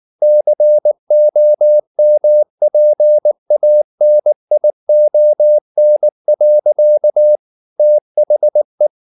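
Morse code, heard as a keyed tone, about 19 wpm, 600 Hz. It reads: COMPANION. THE
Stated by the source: Morse code at 19 wpm from CW med Gnister, reading Ken Follett's novel Never in English.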